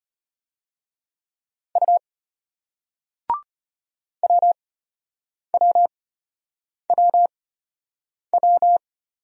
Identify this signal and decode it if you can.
Morse code, heard as a keyed tone, about 34 wpm, spaced as for 8 wpm, 700 Hz. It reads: U WWWW